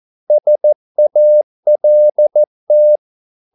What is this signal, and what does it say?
Morse code, heard as a keyed tone, about 14 wpm, 600 Hz. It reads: SALT
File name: Korte ord DK 14wpm